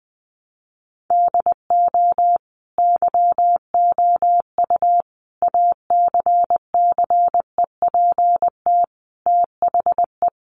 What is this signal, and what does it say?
Morse code, heard as a keyed tone, about 20 wpm, 700 Hz. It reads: DO YOU ACCEPT THE
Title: Sentences from Top 1300 Words 20wpm